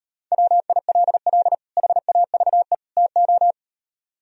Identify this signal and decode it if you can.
Morse code, heard as a keyed tone, about 38 wpm, 700 Hz. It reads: WILL HAVE TO